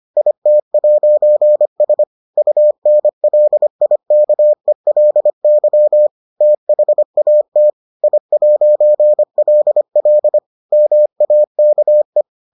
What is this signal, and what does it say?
Morse code, heard as a keyed tone, about 25 wpm, 600 Hz. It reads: IT'S UNLIKELY THAT I'LL MAKE